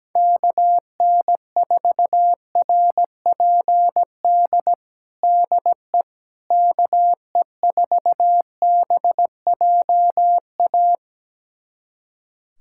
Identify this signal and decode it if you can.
Morse code, heard as a keyed tone, about 17 wpm, 700 Hz. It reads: KN4RPD DE KE4BJA
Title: USA Callsign DE Callsign - 6 Characters - 17wpm